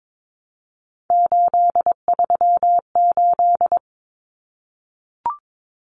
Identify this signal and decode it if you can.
Morse code, heard as a keyed tone, about 22 wpm, 700 Hz. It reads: 838
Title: Numbers - 3-Digit 22wpm